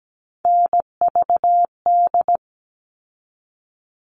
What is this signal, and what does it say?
Morse code, heard as a keyed tone, about 17 wpm, 700 Hz. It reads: NVD
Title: All Cut Numbers - 3-Digit 17wpm